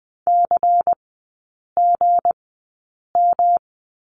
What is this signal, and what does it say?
Morse code, heard as a keyed tone, about 20 wpm, spaced as for 8 wpm, 690 Hz. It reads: CGM
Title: 212 Lesson 34e - Number 8 - Three Characters 20wpm